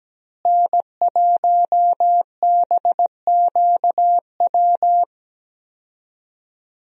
Morse code, heard as a keyed tone, about 17 wpm, 700 Hz. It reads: N1BQW
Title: USA Call Signs - 5 Characters - Repeated 2x - 17wpm